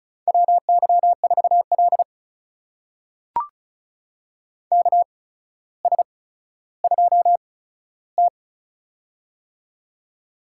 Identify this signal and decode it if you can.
Morse code, heard as a keyed tone, about 36 wpm, spaced as for 15 wpm, 700 Hz. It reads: WY4L KS2T
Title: USA Call Signs - 4 Characters - 8x Character Spacing - 35wpm